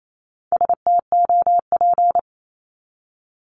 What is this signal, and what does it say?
Morse code, heard as a keyed tone, about 28 wpm, 700 Hz. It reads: STOP
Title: Top 500 Words 28wpm